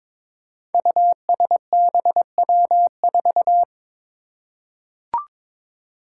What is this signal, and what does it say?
Morse code, heard as a keyed tone, about 22 wpm, 700 Hz. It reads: USBW4
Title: Five Challenging Letter-Number Combinations 22wpm